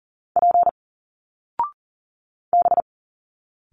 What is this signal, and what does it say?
Morse code, heard as a keyed tone, about 38 wpm, 700 Hz. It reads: P B
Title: Single Letters 40wpm